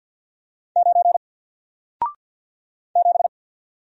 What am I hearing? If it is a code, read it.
Morse code, heard as a keyed tone, about 48 wpm, 700 Hz. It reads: 9 7
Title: Single Character (Letter:Number:Punctuation:Prosign) Alt 50wpm